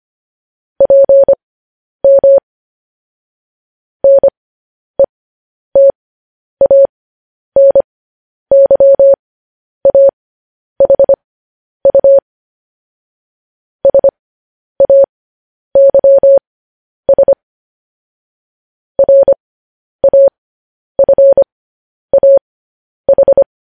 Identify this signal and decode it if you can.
Morse code, heard as a keyed tone, about 25 wpm, spaced as for 10 wpm, 550 Hz. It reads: PM NETANYAHU SAYS RAFAH